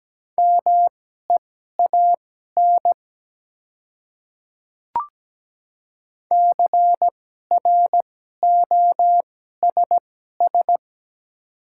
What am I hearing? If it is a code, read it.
Morse code, heard as a keyed tone, about 17 wpm, spaced as for 12 wpm, 690 Hz. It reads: MEAN CROSS
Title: 2x Character Spacing - Top 300 Words - 17wpm